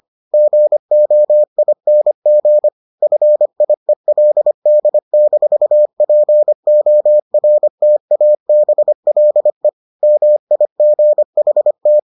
Morse code, heard as a keyed tone, about 25 wpm, 600 Hz. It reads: GOING FIELD-PORTABLE MIGHT